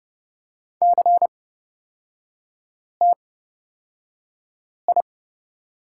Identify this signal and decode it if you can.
Morse code, heard as a keyed tone, about 30 wpm, spaced as for 5 wpm, 700 Hz. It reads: CTI
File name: Lesson 36e - BT - Three Characters 30wpm